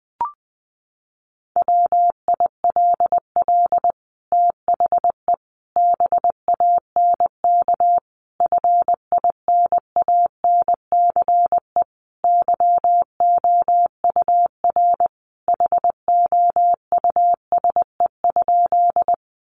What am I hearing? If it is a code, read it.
Morse code, heard as a keyed tone, about 20 wpm, 700 Hz. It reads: WILL THE BANK FINANCE YOUR HOUSE?
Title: Sentences from Top 1500 Words 20wpm